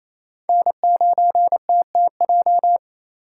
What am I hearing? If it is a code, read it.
Morse code, heard as a keyed tone, about 35 wpm, 700 Hz. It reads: N9TTJ
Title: Speed Racing - USA Call Signs 28wpm